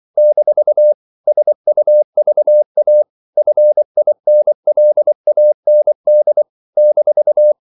Morse code, heard as a keyed tone, about 24 wpm, 600 Hz. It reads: - SUVA FINLAND -
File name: CW / Practice / NationalCapitals24